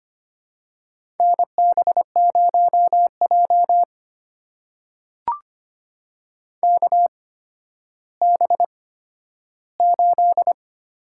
Morse code, heard as a keyed tone, about 25 wpm, spaced as for 12 wpm, 700 Hz. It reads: NB0J KB8
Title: USA Call Signs - 8x Character Spacing - 25wpm